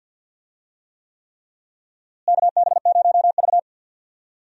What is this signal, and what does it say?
Morse code, heard as a keyed tone, about 46 wpm, 700 Hz. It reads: KB0V
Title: USA Call Signs - 4 Characters - Repeated 3x - 50wpm